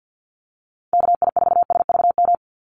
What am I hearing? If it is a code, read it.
Morse code, heard as a keyed tone, about 49 wpm, 700 Hz. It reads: KI4SVM
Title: NA SOTA Chaser Callsigns 50wpm